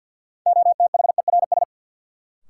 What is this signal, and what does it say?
Morse code, heard as a keyed tone, about 51 wpm, 700 Hz. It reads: OTHERS
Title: Sentences from Top 1500 Words - New Word Drill Alt - 50wpm